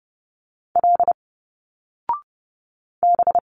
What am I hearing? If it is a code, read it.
Morse code, heard as a keyed tone, about 30 wpm, 700 Hz. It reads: L B
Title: Single Letters 30wpm